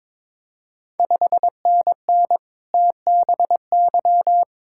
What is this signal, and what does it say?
Morse code, heard as a keyed tone, about 22 wpm, 700 Hz. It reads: 5NN TBY
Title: Ontario QSO Party Exchange 22wpm